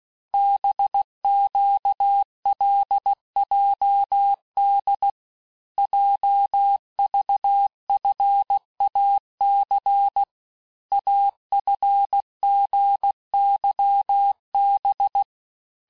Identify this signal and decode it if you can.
Morse code, heard as a keyed tone, about 16 wpm, 800 Hz. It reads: BQLJD JVFAC AFGYB
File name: cours-du-lundi files / cours-1592--teste-chiffre-vitesse-15mpm